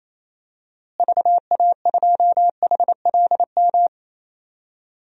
Morse code, heard as a keyed tone, about 27 wpm, 700 Hz. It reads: VA2HLM
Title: DXCC-100 Call Signs - 6 Characters - Repeated 3x - 28wpm